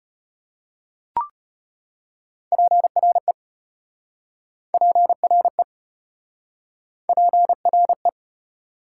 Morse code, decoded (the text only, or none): PRE PRE PRE